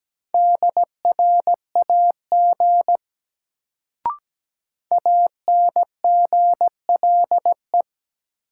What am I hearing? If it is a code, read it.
Morse code, heard as a keyed tone, about 17 wpm, 700 Hz. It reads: DRAG ANGLE
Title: Sentences from Top 1500 Words - New Word Drill Alt - 17wpm